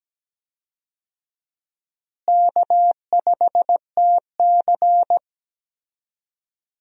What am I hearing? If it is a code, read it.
Morse code, heard as a keyed tone, about 17 wpm, 700 Hz. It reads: K5TC
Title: USA Call Signs - 4 Characters - Repeated 2x - 17wpm